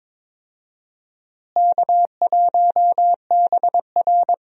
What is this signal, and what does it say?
Morse code, heard as a keyed tone, about 22 wpm, 700 Hz. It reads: K1BR
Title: USA Call Signs - 4 Characters - Repeated 3x - 22wpm